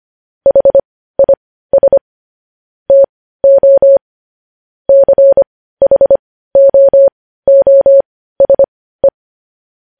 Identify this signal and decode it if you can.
Morse code, heard as a keyed tone, about 25 wpm, spaced as for 15 wpm, 550 Hz. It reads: HIS TO CHOOSE
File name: podcast / News headlines-1x-25x15